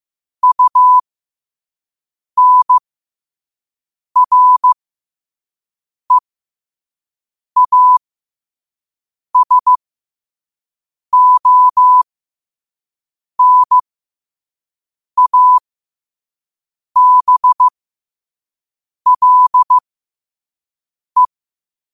Morse code, heard as a keyed tone, about 15 wpm, spaced as for 5 wpm, 1000 Hz. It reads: UNREASONABLE
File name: qotdfiles / 5wpm